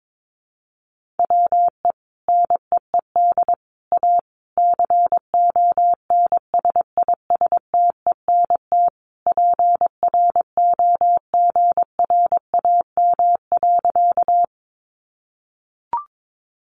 Morse code, heard as a keyed tone, about 22 wpm, 700 Hz. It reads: WE NEED A CONSISTENT PROGRAM.
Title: Sentences from Top 1500 Words 22wpm